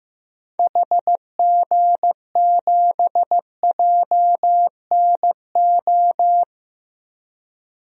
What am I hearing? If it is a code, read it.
Morse code, heard as a keyed tone, about 15 wpm, 700 Hz. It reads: HG7JNO
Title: DXCC-100 Call Signs - 6 Characters - Repeated 2x - 15wpm